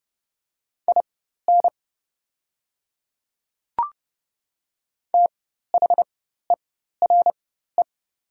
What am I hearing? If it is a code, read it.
Morse code, heard as a keyed tone, about 30 wpm, spaced as for 14 wpm, 690 Hz. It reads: IN THERE
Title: 4x Character Spacing - Top 100 Words - 30wpm